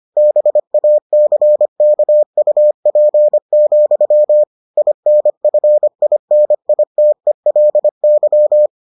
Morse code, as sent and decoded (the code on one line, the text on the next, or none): -... .- -.-. -.- ..- .--. --..-- / .. -. ..-. .. -. .. - . .-.. -.--
BACKUP, INFINITELY